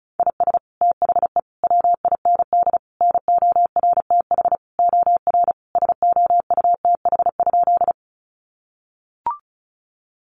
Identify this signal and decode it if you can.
Morse code, heard as a keyed tone, about 34 wpm, 700 Hz. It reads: IS THE WIND NORTH OR SOUTH?
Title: Sentences from Top 400 Words 35wpm